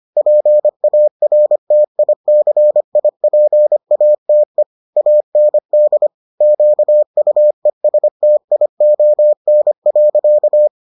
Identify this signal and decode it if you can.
Morse code, heard as a keyed tone, about 25 wpm, 600 Hz. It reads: PARTICIPATE AND QUESTION.